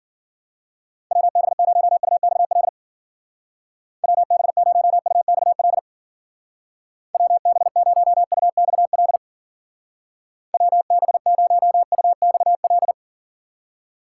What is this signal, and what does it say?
Morse code, heard as a keyed tone, about 52 wpm, 700 Hz. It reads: WB0UXL WB0UXL WB0UXL WB0UXL